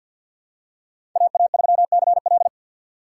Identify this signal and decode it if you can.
Morse code, heard as a keyed tone, about 53 wpm, 700 Hz. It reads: AA3CL
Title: USA Call Signs - 5 Characters - 8x Character Spacing - 50wpm